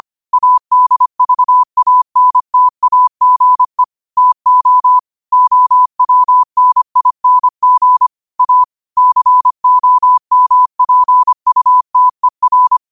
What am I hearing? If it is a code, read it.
Morse code, heard as a keyed tone, about 25 wpm, 1000 Hz. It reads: ADVANTAGE TO OWNING A COMPUTER
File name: qotdfiles / 25wpm